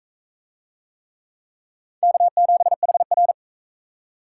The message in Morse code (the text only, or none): K7HR